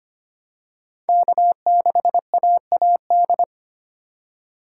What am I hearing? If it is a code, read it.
Morse code, heard as a keyed tone, about 25 wpm, 700 Hz. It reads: K6AAD
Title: USA Call Signs - 5 Characters - 8x Character Spacing - 25wpm